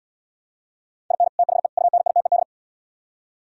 Morse code, heard as a keyed tone, about 77 wpm, 700 Hz. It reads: IN THE UNITED